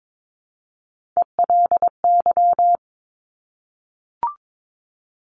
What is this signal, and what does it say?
Morse code, heard as a keyed tone, about 22 wpm, 700 Hz. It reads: ELY
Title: Top Three Letter Combinations in Words 22wpm